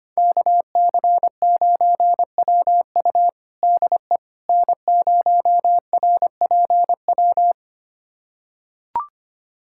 Morse code, heard as a keyed tone, about 25 wpm, 700 Hz. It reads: KC9WU DE N0RPW
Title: USA Callsign DE Callsign - 5 Characters - 25wpm